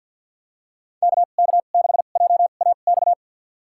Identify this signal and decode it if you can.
Morse code, heard as a keyed tone, about 52 wpm, spaced as for 36 wpm, 700 Hz. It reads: KK6JAX
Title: USA Call Signs - 2x Character Spacing - 50wpm